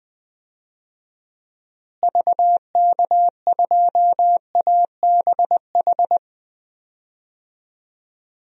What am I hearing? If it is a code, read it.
Morse code, heard as a keyed tone, about 20 wpm, 700 Hz. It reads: VK2ABH
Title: DXCC-100 Call Signs - 6 Characters - Repeated 2x - 20wpm